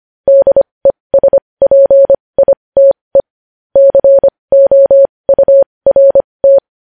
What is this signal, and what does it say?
Morse code, heard as a keyed tone, about 25 wpm, spaced as for 20 wpm, 550 Hz. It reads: DESPITE COURT
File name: podcast / News headlines-1x-25x20